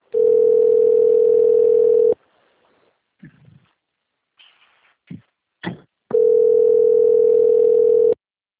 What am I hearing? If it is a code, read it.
Ringback tone (North America), 440 + 480 Hz.